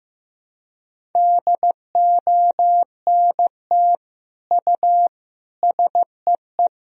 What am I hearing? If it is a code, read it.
Morse code, heard as a keyed tone, about 16 wpm, 700 Hz. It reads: DONT U SEE
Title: Sentences from Top 300 Words and Abbreviations 15wpm